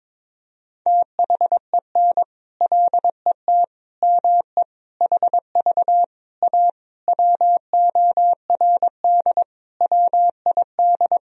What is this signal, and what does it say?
Morse code, heard as a keyed tone, about 21 wpm, 700 Hz. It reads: THEN LET ME HV A WORD WID
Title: Sentences from Top 300 Words and Abbreviations 22wpm